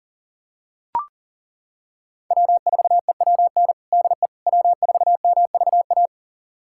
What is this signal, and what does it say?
Morse code, heard as a keyed tone, about 40 wpm, 700 Hz. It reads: W4EWN DE W4MVA
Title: USA Callsign DE Callsign - 5 Characters - 40wpm